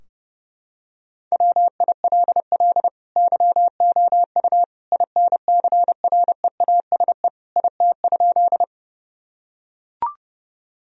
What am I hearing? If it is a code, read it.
Morse code, heard as a keyed tone, about 30 wpm, 700 Hz. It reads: WILL YOU INCREASE IT?
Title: Sentences from Top 1000 Words - Review Part 4 - 30wpm